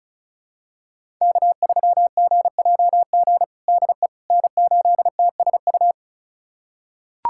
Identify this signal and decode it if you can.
Morse code, heard as a keyed tone, about 34 wpm, 700 Hz. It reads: K3GJG DE N8TSU
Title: USA Callsign DE Callsign - 5 Characters - 35wpm